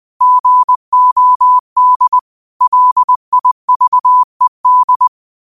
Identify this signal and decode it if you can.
Morse code, heard as a keyed tone, about 20 wpm, 1000 Hz. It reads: GOD LIVED
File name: qotdfiles / 20wpm